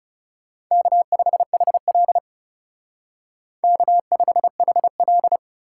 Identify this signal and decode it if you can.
Morse code, heard as a keyed tone, about 37 wpm, 700 Hz. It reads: K5HL K5HL
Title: Speed Racing - USA 1x2 Call Signs 30wpm